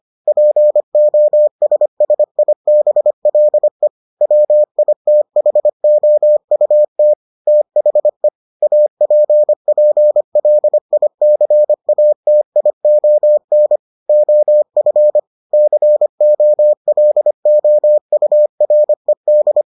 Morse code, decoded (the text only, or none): POSSIBLE WITHOUT THE APPLICATION OF COLOURED